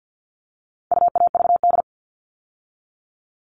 Morse code, heard as a keyed tone, about 50 wpm, 700 Hz. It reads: UAVD
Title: All Cut Numbers - 4-Digit 50wpm